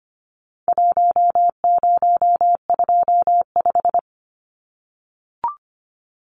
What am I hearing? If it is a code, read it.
Morse code, heard as a keyed tone, about 25 wpm, 700 Hz. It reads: 1025
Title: Numbers - Contest Style - 4-Digit 25wpm